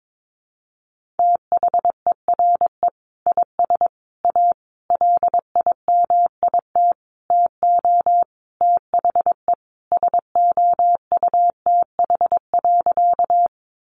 Morse code, heard as a keyed tone, about 22 wpm, 700 Hz. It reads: THERE IS A LIMIT TO THE SOUTH.
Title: Sentences from Top 1500 Words 22wpm